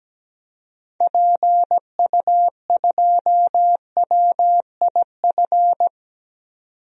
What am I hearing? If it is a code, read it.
Morse code, heard as a keyed tone, about 17 wpm, 700 Hz. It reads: PU2WIF
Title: DXCC-100 Call Signs - 6 Characters - 17wpm